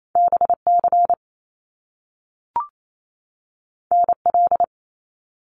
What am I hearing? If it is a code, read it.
Morse code, heard as a keyed tone, about 28 wpm, 700 Hz. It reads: BC NL
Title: Canadian Provinces and Territories 28wpm